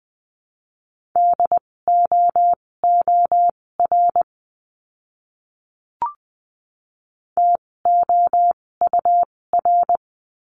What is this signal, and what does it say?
Morse code, heard as a keyed tone, about 20 wpm, spaced as for 16 wpm, 700 Hz. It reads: DOOR TOUR
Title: 256 Lesson 42a - Words 12-20wpm